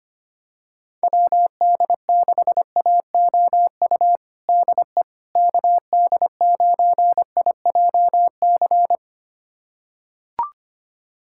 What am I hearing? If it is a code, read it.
Morse code, heard as a keyed tone, about 25 wpm, 700 Hz. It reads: WD6AOU DE KD9IJC
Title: USA Callsign DE Callsign - 6 Characters - 25wpm